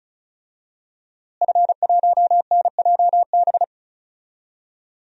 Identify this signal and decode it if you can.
Morse code, heard as a keyed tone, about 36 wpm, 700 Hz. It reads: F1NJB
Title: DXCC-100 Call Signs - 5 Characters - Repeated 2x - 35wpm